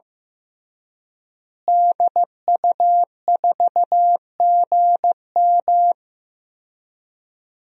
Morse code, heard as a keyed tone, about 15 wpm, 700 Hz. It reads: DU4GM
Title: DXCC-100 Call Signs - 5 Characters - Repeated 3x - 15wpm